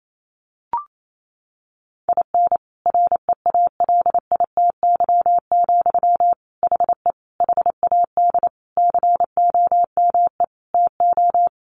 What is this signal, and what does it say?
Morse code, heard as a keyed tone, about 28 wpm, 700 Hz. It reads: IN REALITY, HE HAD COME TO
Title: Sentences from Top 1500 Words 28wpm